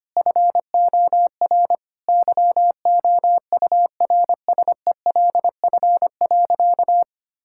Morse code, heard as a keyed tone, about 25 wpm, 700 Hz. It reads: FOR YOURSELF.
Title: Sentences from Top 1000 Words - Review Part 6 - 25wpm